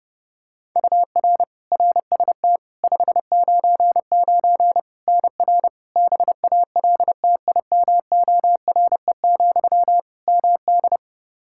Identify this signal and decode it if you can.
Morse code, heard as a keyed tone, about 30 wpm, 700 Hz. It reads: UR RST 599 NR BALTIMORE, MD
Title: Practice QSO - Exchange 1 - Beginning - 30wpm